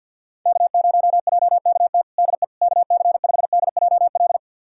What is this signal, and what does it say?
Morse code, heard as a keyed tone, about 50 wpm, 700 Hz. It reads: K0JKT DE KC5DJL
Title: USA Callsign DE Callsign 50wpm